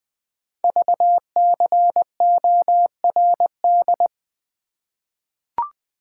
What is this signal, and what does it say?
Morse code, heard as a keyed tone, about 20 wpm, 700 Hz. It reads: VCORD
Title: Code Groups - 5 Letters - 20wpm